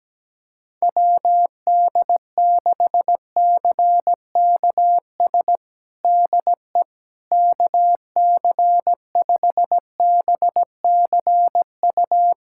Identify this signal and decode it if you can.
Morse code, heard as a keyed tone, about 17 wpm, 700 Hz. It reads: WD6CKS DE KC5BCU
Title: USA Callsign DE Callsign - 6 Characters - 17wpm